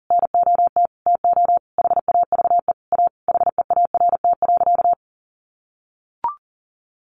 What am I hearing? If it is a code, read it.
Morse code, heard as a keyed tone, about 40 wpm, 700 Hz. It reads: NOT TO HAVE A HEART.
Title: Sentences from Top 1500 Words 40wpm